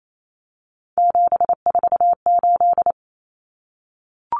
Morse code, heard as a keyed tone, about 28 wpm, 700 Hz. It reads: 748
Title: Common Cut Numbers - 3-Digit 28wpm